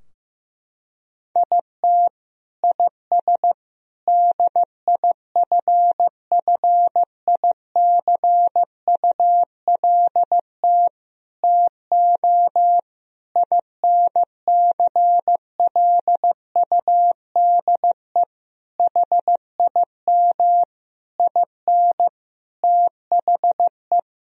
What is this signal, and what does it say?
Morse code, heard as a keyed tone, about 15 wpm, 700 Hz. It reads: IT IS DIFFICULT TO INCLUDE HIM IN THE
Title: Sentences from Top 1000 Words - Review Part 3 - 15wpm